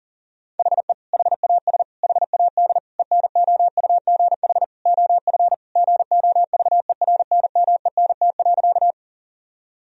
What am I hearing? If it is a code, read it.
Morse code, heard as a keyed tone, about 40 wpm, 700 Hz. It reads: HE HAS HAD ENOUGH OF GOVERNMENT.
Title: Sentences from Top 1000 Words - Review Part 3 - 40wpm